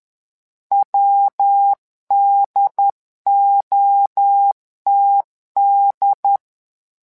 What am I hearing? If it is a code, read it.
Morse code, heard as a keyed tone, about 10 wpm, 800 Hz. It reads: WDOTD
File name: LEZIONI / 3 MWRO / 3 MWRO 10wpm 2148